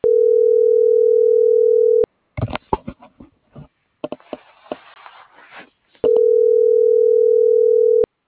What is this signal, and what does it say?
Ringback tone (North America), 440 + 480 Hz.